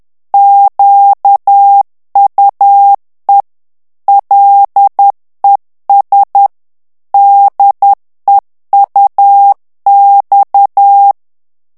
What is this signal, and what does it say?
Morse code, heard as a keyed tone, about 10 wpm, 800 Hz. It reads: QUE LES DEUX
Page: Français - texte en clair 1 - vitesse : 10 mots minute